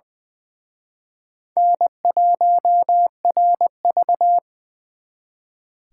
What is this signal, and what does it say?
Morse code, heard as a keyed tone, about 24 wpm, 700 Hz. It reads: N1RV
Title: Speed Racing - USA 1x2 Call Signs 20wpm